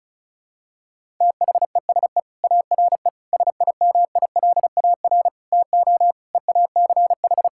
Morse code, heard as a keyed tone, about 34 wpm, 700 Hz. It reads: THESE ARE SIMILAR TO EACH
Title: Sentences from Top 1000 Words - Review Part 5 - 35wpm